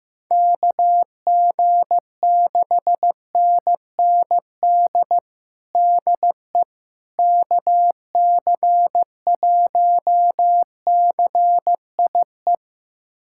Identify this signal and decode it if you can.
Morse code, heard as a keyed tone, about 15 wpm, 700 Hz. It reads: KG6NND DE KC1CIE
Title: USA Callsign DE Callsign - 6 Characters - 15wpm